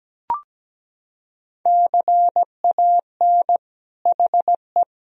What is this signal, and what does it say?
Morse code, heard as a keyed tone, about 17 wpm, 700 Hz. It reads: CAN HE
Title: Sentences from Top 1000 Words - Review Part 4 - 17wpm